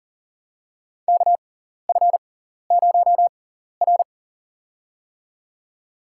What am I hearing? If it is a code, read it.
Morse code, heard as a keyed tone, about 40 wpm, spaced as for 14 wpm, 700 Hz. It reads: KF0R
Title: USA Call Signs - 4 Characters - 6x Character Spacing - 40wpm